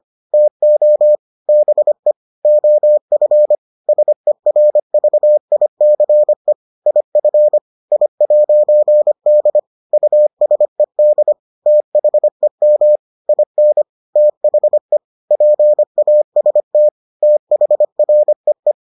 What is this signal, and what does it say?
Morse code, heard as a keyed tone, about 25 wpm, 600 Hz. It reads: TO BE OF SERVICE IF I'D USED THEM IN THE PAST THREE